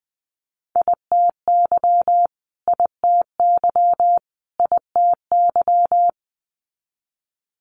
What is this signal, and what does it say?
Morse code, heard as a keyed tone, about 19 wpm, 700 Hz. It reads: ITY ITY ITY
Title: Multi-letter Phonemes Repeated 3x 20wpm